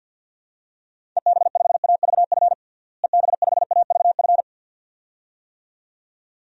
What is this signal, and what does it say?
Morse code, heard as a keyed tone, about 51 wpm, 700 Hz. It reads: EB5AVF EB5AVF
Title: DXCC-100 Call Signs - 6 Characters - Repeated 2x - 50wpm